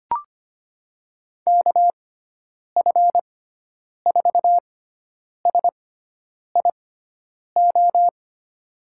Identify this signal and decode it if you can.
Morse code, heard as a keyed tone, about 25 wpm, spaced as for 9 wpm, 700 Hz. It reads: KF4SIO
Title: USA Call Signs - 6 Characters - 6x Character Spacing - 25wpm